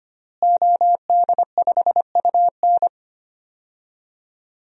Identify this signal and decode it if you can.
Morse code, heard as a keyed tone, about 25 wpm, 700 Hz. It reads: OD5UN
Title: DXCC-100 Call Signs - 5 Characters - Repeated 3x - 25wpm